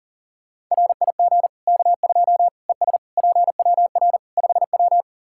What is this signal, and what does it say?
Morse code, heard as a keyed tone, about 40 wpm, 700 Hz. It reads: RIG K2 ES PWR 5W
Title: Practice QSO - Exchange 2 - Info - 40wpm